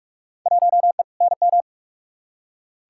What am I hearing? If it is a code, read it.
Morse code, heard as a keyed tone, about 42 wpm, 700 Hz. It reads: 1E NM
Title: Field Day Exchanges - 45wpm